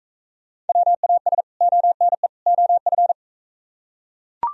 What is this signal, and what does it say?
Morse code, heard as a keyed tone, about 45 wpm, 700 Hz. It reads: WAS ONE OF